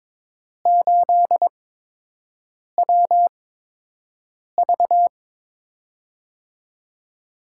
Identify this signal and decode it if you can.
Morse code, heard as a keyed tone, about 22 wpm, spaced as for 6 wpm, 700 Hz. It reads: 8WV
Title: USA Call Signs - 4 Characters - 8x Character Spacing - 22wpm